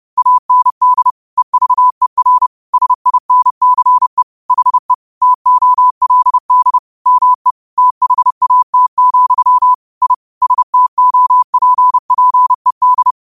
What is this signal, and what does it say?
Morse code, heard as a keyed tone, about 30 wpm, 1000 Hz. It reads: AND EVER SINCE HE TOLD ME THAT, I STOPPED
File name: qotdfiles / 30wpm